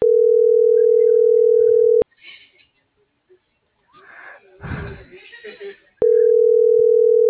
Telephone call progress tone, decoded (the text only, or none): ringback tone